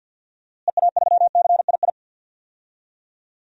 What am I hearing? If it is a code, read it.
Morse code, heard as a keyed tone, about 50 wpm, 700 Hz. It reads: EA3CII
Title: DXCC-100 Call Signs - 6 Characters - Repeated 2x - 50wpm